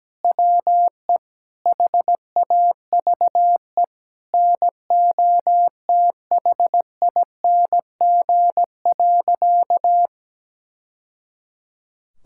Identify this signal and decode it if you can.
Morse code, heard as a keyed tone, about 17 wpm, 700 Hz. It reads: WE HAVE NOTHING.
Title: Sentences from Top 1000 Words - Review Part 5 - 17wpm